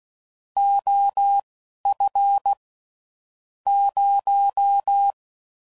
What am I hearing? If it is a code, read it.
Morse code, heard as a keyed tone, about 16 wpm, spaced as for 11 wpm, 800 Hz. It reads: OF 0
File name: CW Code Groups @10WPM / Technical Subjects / Gates Intro